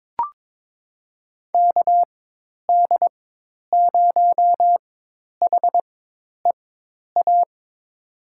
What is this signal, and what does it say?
Morse code, heard as a keyed tone, about 22 wpm, spaced as for 10 wpm, 700 Hz. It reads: KD0HEA
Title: USA Call Signs - 6 Characters - 4x Character Spacing - 22wpm